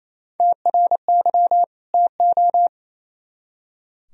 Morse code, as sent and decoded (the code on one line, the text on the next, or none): - .-. -.-- / - ---
TRY TO